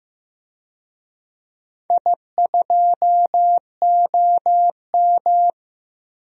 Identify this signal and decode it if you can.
Morse code, heard as a keyed tone, about 15 wpm, 700 Hz. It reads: I2OM
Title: DXCC-100 Call Signs - 4 Characters - Repeated 3x - 15wpm